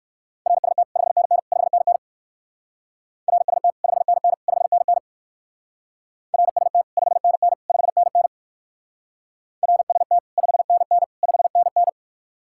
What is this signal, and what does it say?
Morse code, heard as a keyed tone, about 56 wpm, 700 Hz. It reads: RST 5NN 5NN RST 5NN 5NN RST 5NN 5NN RST 5NN 5NN
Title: Speed-Racing Common QSO Phrases - 45wpm